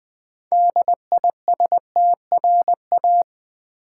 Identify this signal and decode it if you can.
Morse code, heard as a keyed tone, about 20 wpm, 700 Hz. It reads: DISTRA
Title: Build A Word - 2 Letters At A Time - 8-Letter Words - 20wpm